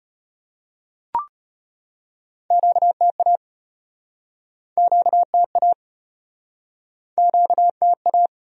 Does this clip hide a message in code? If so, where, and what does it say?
Yes: Morse code from 2.5 to 8.3 s, heard as a keyed tone, about 37 wpm, 700 Hz. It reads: QTA QTA QTA